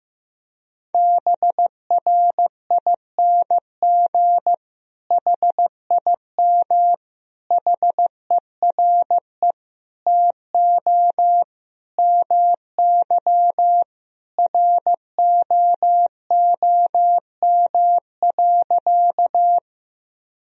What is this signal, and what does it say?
Morse code, heard as a keyed tone, about 16 wpm, 700 Hz. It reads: BRING HIM HERE TO MY ROOM.
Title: Sentences from Top 1000 Words - Review Part 3 - 15wpm